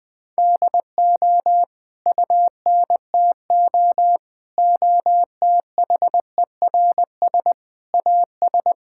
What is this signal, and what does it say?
Morse code, heard as a keyed tone, about 20 wpm, 700 Hz. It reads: DO UNTO OTHERS AS
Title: Sets of 8 Words - 20wpm